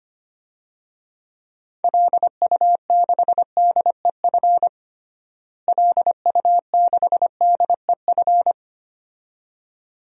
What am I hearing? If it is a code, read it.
Morse code, heard as a keyed tone, about 25 wpm, 700 Hz. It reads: LU6DEF LU6DEF
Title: DXCC-100 Call Signs - 6 Characters - Repeated 2x - 25wpm